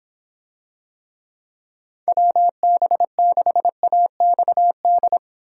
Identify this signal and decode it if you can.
Morse code, heard as a keyed tone, about 26 wpm, 700 Hz. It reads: WB6AXD